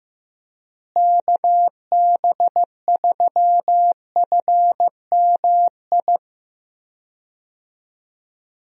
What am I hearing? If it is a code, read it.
Morse code, heard as a keyed tone, about 16 wpm, 700 Hz. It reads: KB3FMI
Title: USA Call Signs - 6 Characters - Repeated 2x - 15wpm